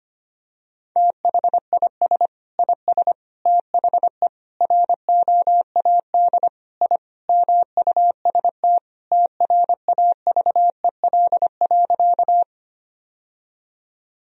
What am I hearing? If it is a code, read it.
Morse code, heard as a keyed tone, about 25 wpm, 700 Hz. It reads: THIS IS THE ROAD I MUST TRAVEL.
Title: Sentences from Top 1000 Words - Review Part 2 - 25wpm